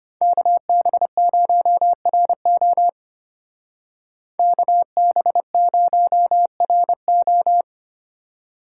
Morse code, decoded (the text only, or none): KB0RO KB0RO